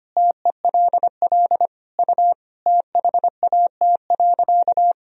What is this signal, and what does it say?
Morse code, heard as a keyed tone, about 25 wpm, 700 Hz. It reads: TELL U THAT.
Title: Sentences from Top 300 Words and Abbreviations 25wpm